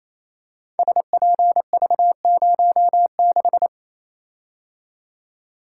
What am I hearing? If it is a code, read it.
Morse code, heard as a keyed tone, about 28 wpm, 700 Hz. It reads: SPV06
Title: Code Groups - 5 Letter:Numbers - 28wpm